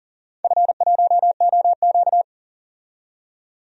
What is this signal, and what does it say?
Morse code, heard as a keyed tone, about 40 wpm, 700 Hz. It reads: F1OQ
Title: DXCC-100 Call Signs - 4 Characters - 40wpm